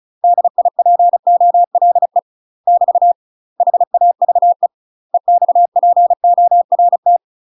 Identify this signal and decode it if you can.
Morse code, heard as a keyed tone, about 36 wpm, 700 Hz. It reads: DIPOLE <BT> HAVE EXPORT
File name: UK MASTER QSOs TRAINING / 35WPM / UK MASTER QSOs 0020